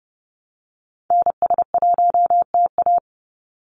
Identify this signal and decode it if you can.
Morse code, heard as a keyed tone, about 30 wpm, 700 Hz. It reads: NS1TA
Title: NA SOTA Activator Callsigns 30wpm